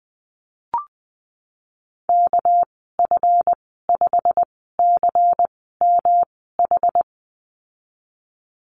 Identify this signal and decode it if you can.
Morse code, heard as a keyed tone, about 19 wpm, spaced as for 14 wpm, 700 Hz. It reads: KF5CMH
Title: 238 Lesson 38g - Letter q - Call Signs 20wpm